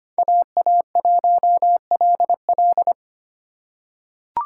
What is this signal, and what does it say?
Morse code, heard as a keyed tone, about 25 wpm, 700 Hz. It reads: AA1LL
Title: USA Call Signs - 5 Characters - 4x Character Spacing - 25wpm